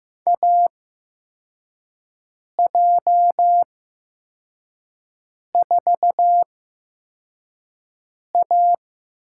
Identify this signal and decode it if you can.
Morse code, heard as a keyed tone, about 15 wpm, spaced as for 4 wpm, 700 Hz. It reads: AJ4A
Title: USA Call Signs - 4 Characters - 8x Character Spacing - 15wpm